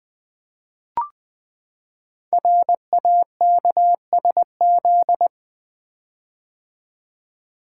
Morse code, heard as a keyed tone, about 20 wpm, 700 Hz. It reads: RAKSZ
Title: Code Groups - 5 Letter:Numbers - 20wpm